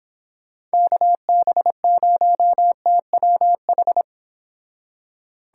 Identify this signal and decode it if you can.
Morse code, heard as a keyed tone, about 29 wpm, 700 Hz. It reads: KB0TWH